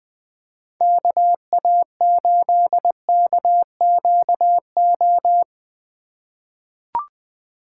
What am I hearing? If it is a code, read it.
Morse code, heard as a keyed tone, about 20 wpm, 700 Hz. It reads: KA8KQO
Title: USA Call Signs - 2x Character Spacing - 20wpm